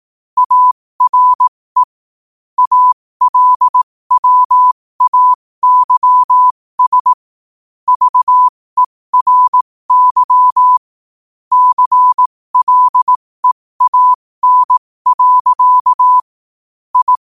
Morse code, heard as a keyed tone, about 18 wpm, spaced as for 15 wpm, 1000 Hz. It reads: ARE ALWAYS VERY CLEAN. I
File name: qotdfiles / 15wpm